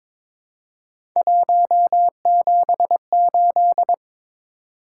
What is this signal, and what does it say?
Morse code, heard as a keyed tone, about 22 wpm, 700 Hz. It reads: 178
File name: practice / 22wpm / Three Letter-Number Combinations - 22wpm